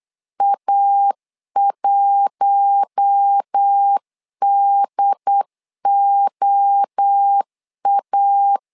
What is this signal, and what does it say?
Morse code, heard as a keyed tone, about 8 wpm, 800 Hz. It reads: A1DOA
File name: LEZIONI / 4 Q1 8wpm 3035